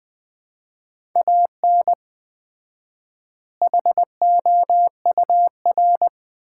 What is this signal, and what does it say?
Morse code, heard as a keyed tone, about 20 wpm, 700 Hz. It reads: AN HOUR
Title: 4x Word-Spacing - Sets of 5 Words - 20wpm